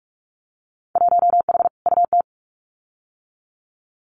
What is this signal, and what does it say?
Morse code, heard as a keyed tone, about 46 wpm, 700 Hz. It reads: 1H UT
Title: Winter Field Day Exchanges 45wpm